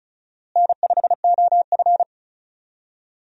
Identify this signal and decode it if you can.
Morse code, heard as a keyed tone, about 34 wpm, 700 Hz. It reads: N5OF
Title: USA Call Signs - 4 Characters - Repeated 3x - 35wpm